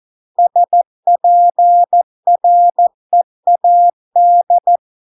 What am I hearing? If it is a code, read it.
Morse code, heard as a keyed tone, about 14 wpm, 700 Hz. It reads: SPREAD